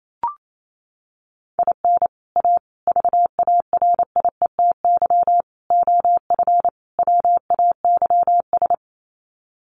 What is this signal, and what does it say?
Morse code, heard as a keyed tone, about 28 wpm, 700 Hz. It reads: IN A VARIETY OF WAYS
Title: Sets of 5 Words Encore 28wpm